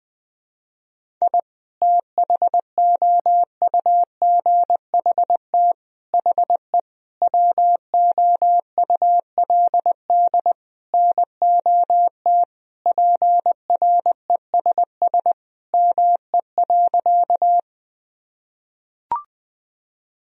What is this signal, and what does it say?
Morse code, heard as a keyed tone, about 21 wpm, 700 Hz. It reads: I THOUGHT HE WOULD NOT PRESS ME.
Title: Sentences from Top 1000 Words - Review Part 1 - 20wpm